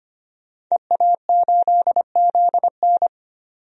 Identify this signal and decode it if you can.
Morse code, heard as a keyed tone, about 25 wpm, 700 Hz. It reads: EA8ZN
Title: DXCC-100 Call Signs - 5 Characters - 25wpm